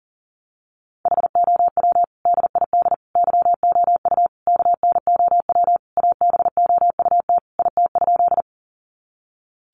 Morse code, heard as a keyed tone, about 40 wpm, 700 Hz. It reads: HOW DID YOU KNOW ABOUT IT?
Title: Sentences from Top 100 Words 40wpm